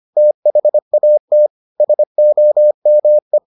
Morse code, heard as a keyed tone, about 25 wpm, 600 Hz. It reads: THAT SOME